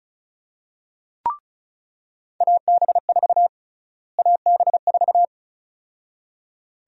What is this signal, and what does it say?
Morse code, heard as a keyed tone, about 36 wpm, 700 Hz. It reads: AB4 AB4
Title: USA Call Sign Prefixes - Repeated 2x - 35wpm